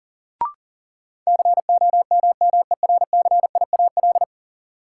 Morse code, heard as a keyed tone, about 40 wpm, 700 Hz. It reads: COMMERCIAL
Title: Sentences from Top 1000 Words - Review Part 5 - New Word Drill - 40wpm